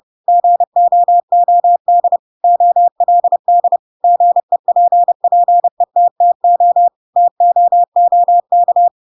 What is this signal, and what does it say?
Morse code, heard as a keyed tone, about 30 wpm, 700 Hz. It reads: GOOD OLD GEPPETTO TOOK
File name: Pinocchio / Pinocchio ch7 30 30 ik1hge